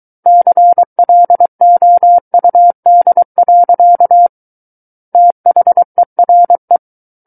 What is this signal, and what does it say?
Morse code, heard as a keyed tone, about 23 wpm, 700 Hz. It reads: CLOUD. THERE
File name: Book One / 14-Chapter 14 - 23 WPM